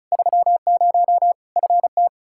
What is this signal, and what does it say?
Morse code, heard as a keyed tone, about 36 wpm, 700 Hz. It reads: 30 FT
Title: Practice QSO - Exchange 2 - Info - 35wpm